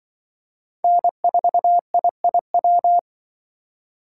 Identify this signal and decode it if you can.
Morse code, heard as a keyed tone, about 24 wpm, 700 Hz. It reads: N4IIW
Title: Speed Racing - USA 1x3 Call Signs 20wpm